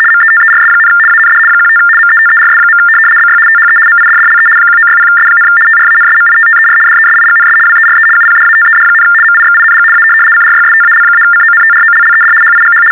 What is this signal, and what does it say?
RTTY, about 75 baud. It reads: METERS, AND 15 METERS. BY THE MIDDLE 1960S, TRANSMITTER DESIGNS WERE UPDATED, MIXING A CRYSTAL-CONTROLLED HIGH
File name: teletype-RTTY-75